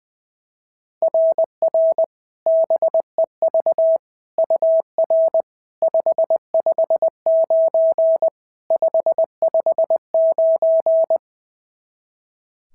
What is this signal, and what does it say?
Morse code, heard as a keyed tone, about 20 wpm, 660 Hz. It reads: RR BEV UR 559 559
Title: SOTA QSO Practice 20wpm